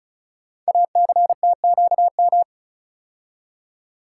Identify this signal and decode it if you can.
Morse code, heard as a keyed tone, about 34 wpm, 700 Hz. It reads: ACTQM
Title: Code Groups - 5 Letters - 35wpm